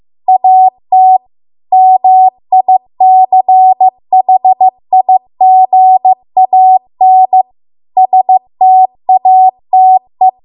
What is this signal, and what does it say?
Morse code, heard as a keyed tone, about 15 wpm, 750 Hz. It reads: AT MICHIGAN STATE
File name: Archive / 15 WPM / 15WPM